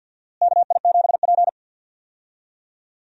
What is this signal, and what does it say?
Morse code, heard as a keyed tone, about 47 wpm, 700 Hz. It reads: KI7P
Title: USA Call Signs - 4 Characters - Repeated 2x - 50wpm